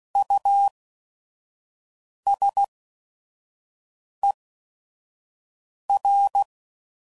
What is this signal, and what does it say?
Morse code, heard as a keyed tone, about 16 wpm, spaced as for 5 wpm, 800 Hz. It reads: USER
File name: Prefixes / By First Letter / Us